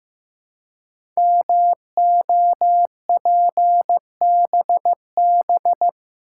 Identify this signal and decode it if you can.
Morse code, heard as a keyed tone, about 15 wpm, 690 Hz. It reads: MOPBB
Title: Code Groups - 5 Letters - 15wpm